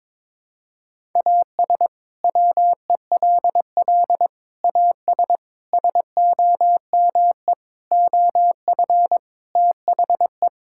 Morse code, heard as a keyed tone, about 22 wpm, 700 Hz. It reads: AS WELL AS SOME OF THE
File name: practice / 22wpm / Sets of 6 Words - 22wpm